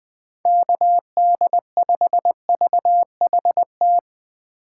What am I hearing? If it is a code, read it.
Morse code, heard as a keyed tone, about 20 wpm, 700 Hz. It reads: KD5VHT
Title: USA Call Signs - 6 Characters - 6x Character Spacing - 20wpm